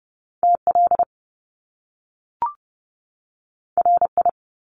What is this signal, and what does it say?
Morse code, heard as a keyed tone, about 30 wpm, 700 Hz. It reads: TL RI
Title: Top Two Letter Combinations in Words 30wpm